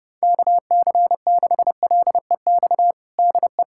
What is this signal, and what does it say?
Morse code, heard as a keyed tone, about 32 wpm, 700 Hz. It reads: KC6LEX DE
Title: USA Callsign DE Callsign 30wpm